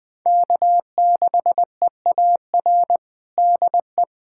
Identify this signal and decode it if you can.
Morse code, heard as a keyed tone, about 20 wpm, 700 Hz. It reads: K6EAR DE
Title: USA Callsign DE Callsign - 5 Characters - 20wpm